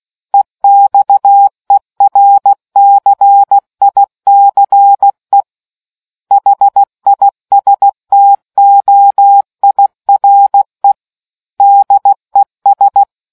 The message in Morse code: EXERCICE HISTOIRE DES

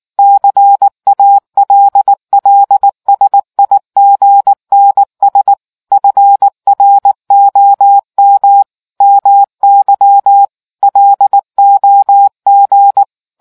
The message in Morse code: CALLSIGNS FROM MY LOG

